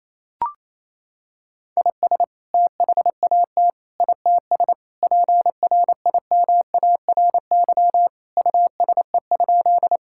Morse code, heard as a keyed tone, about 28 wpm, 700 Hz. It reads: IS THAT ITS PRIMARY USE?
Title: Sentences from Top 1000 Words - Review Part 5 - 28wpm